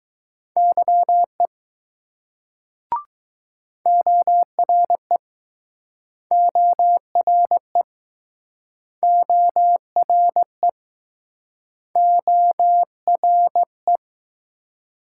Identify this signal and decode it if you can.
Morse code, heard as a keyed tone, about 20 wpm, 700 Hz. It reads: YE ORE ORE ORE ORE